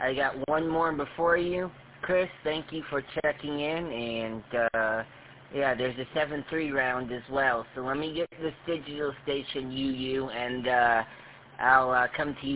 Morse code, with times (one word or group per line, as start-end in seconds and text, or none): none